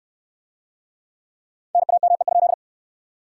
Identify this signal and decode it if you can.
Morse code, heard as a keyed tone, about 61 wpm, 700 Hz. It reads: NAME?